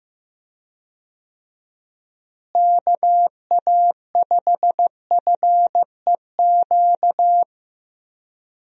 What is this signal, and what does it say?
Morse code, heard as a keyed tone, about 15 wpm, 700 Hz. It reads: KA5FEQ
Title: USA Call Signs - 6 Characters - Repeated 3x - 15wpm